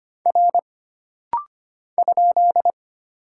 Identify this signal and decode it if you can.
Morse code, heard as a keyed tone, about 25 wpm, 700 Hz. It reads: R ?
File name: Single Character (Letter:Number:Punctuation:Prosign) - Rapid-Fire - 25wpm